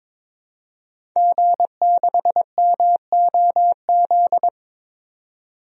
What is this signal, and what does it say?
Morse code, heard as a keyed tone, about 22 wpm, 700 Hz. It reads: G6MOZ
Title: DXCC-100 Call Signs - 5 Characters - 22wpm